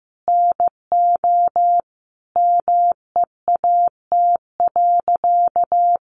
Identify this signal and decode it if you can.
Morse code, heard as a keyed tone, about 14 wpm, 700 Hz. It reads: NO MEAT.
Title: Sentences from Top 1300 Words 15wpm